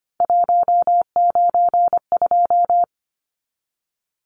Morse code, heard as a keyed tone, about 25 wpm, 700 Hz. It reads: 192
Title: Numbers - 3-Digit 25wpm